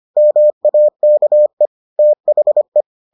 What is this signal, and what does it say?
Morse code, heard as a keyed tone, about 25 wpm, 600 Hz. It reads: MAKE THE